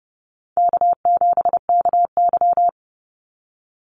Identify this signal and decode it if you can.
Morse code, heard as a keyed tone, about 30 wpm, 700 Hz. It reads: K7KY
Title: Call Signs - 30wpm